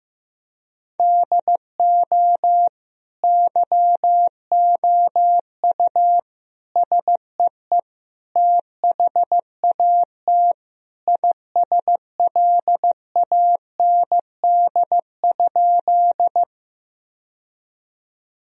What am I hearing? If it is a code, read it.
Morse code, heard as a keyed tone, about 15 wpm, 700 Hz. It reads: DO YOU SEE THAT ISLAND?
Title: Sentences from Top 1000 Words - Review Part 3 - 15wpm